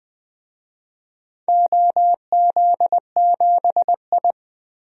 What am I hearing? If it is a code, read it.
Morse code, heard as a keyed tone, about 20 wpm, 700 Hz. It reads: OZ7I
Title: DXCC-100 Call Signs - 4 Characters - 20wpm